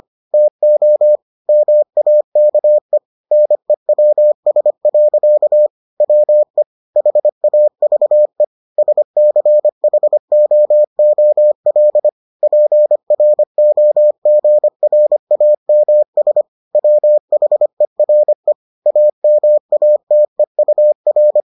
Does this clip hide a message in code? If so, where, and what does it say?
Yes: Morse code from 0.3 to 21.4 s, heard as a keyed tone, about 25 wpm, 600 Hz. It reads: TO MAKE NEWS. WE HAVE SCHOOL PROGRAMS WHERE AMATEUR